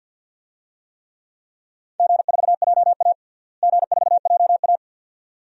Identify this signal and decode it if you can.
Morse code, heard as a keyed tone, about 53 wpm, 700 Hz. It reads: G4JA G4JA
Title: DXCC-100 Call Signs - 4 Characters - Repeated 2x - 50wpm